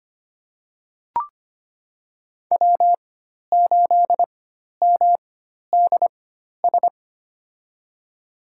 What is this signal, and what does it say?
Morse code, heard as a keyed tone, about 25 wpm, spaced as for 12 wpm, 700 Hz. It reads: W8MDS
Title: USA Call Signs - 5 Characters - 4x Character Spacing - 25wpm